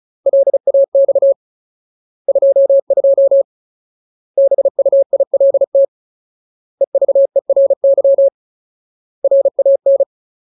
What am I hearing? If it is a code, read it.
Morse code, heard as a keyed tone, about 34 wpm, 550 Hz. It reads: LAX 22 BUILT EVERY RAN
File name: Ultimate Word List / 35WPM Ultimate